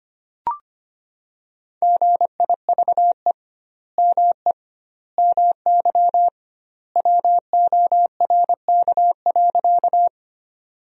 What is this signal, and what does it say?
Morse code, heard as a keyed tone, about 25 wpm, 700 Hz. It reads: GIVE ME MY WORK.
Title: 2x Word-Spacing - Sentences from Top 200 Words - 25wpm